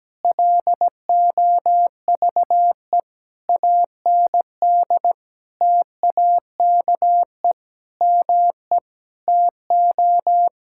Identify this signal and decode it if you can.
Morse code, heard as a keyed tone, about 17 wpm, 700 Hz. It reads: LOVE AND TAKE ME TO
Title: Sentences from Top 1000 Words - Review Part 6 - 17wpm